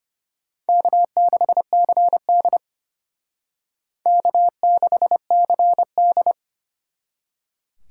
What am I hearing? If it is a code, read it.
Morse code, heard as a keyed tone, about 32 wpm, 700 Hz. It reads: K6CD K6CD